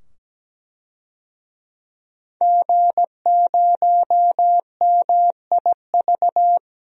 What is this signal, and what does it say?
Morse code, heard as a keyed tone, about 17 wpm, 700 Hz. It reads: G0MIV
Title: DXCC-100 Call Signs - 5 Characters - 17wpm